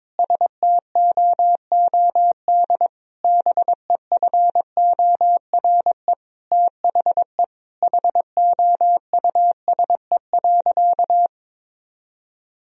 Morse code, heard as a keyed tone, about 22 wpm, 700 Hz. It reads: STOOD BEFORE THE HOUSE.
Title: Sentences from Top 1000 Words - Review Part 2 - 22wpm